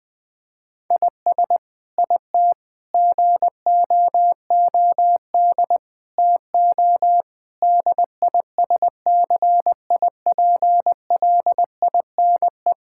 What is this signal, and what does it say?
Morse code, heard as a keyed tone, about 20 wpm, 700 Hz. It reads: IS IT GOOD TO DISCIPLINE
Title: Sentences from Top 1000 Words - Review Part 6 - 20wpm